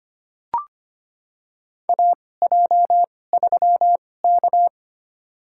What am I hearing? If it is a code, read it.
Morse code, heard as a keyed tone, about 25 wpm, spaced as for 18 wpm, 700 Hz. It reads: AJ3K
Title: USA Call Signs - 4 Characters - 2x Character Spacing - 25wpm